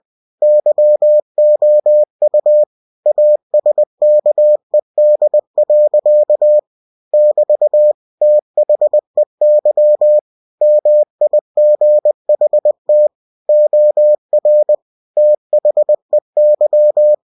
Morse code, heard as a keyed tone, about 20 wpm, 600 Hz. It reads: YOU ASKED. <BT> THEY MIGHT OR THEY
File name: cw-books / winnie-the-pooh / winnie-20wpm-page-0006